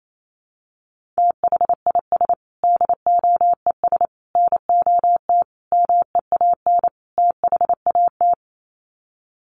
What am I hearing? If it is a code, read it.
Morse code, heard as a keyed tone, about 28 wpm, 700 Hz. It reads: THIS DOES NOT MEAN THAT
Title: Sets of 5 Words Encore2 28wpm